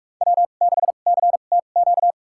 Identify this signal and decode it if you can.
Morse code, heard as a keyed tone, about 44 wpm, spaced as for 32 wpm, 700 Hz. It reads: W6CTQ
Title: USA Call Signs - 5 Characters - 2x Character Spacing - 45wpm